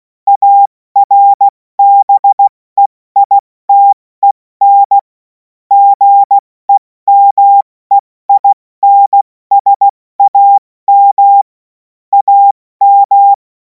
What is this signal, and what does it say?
Morse code, heard as a keyed tone, about 16 wpm, spaced as for 14 wpm, 800 Hz. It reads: ARBEITEN GEMEINSAM AM
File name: Morsekurs / Witze / Witz 07. April 2026 70 F80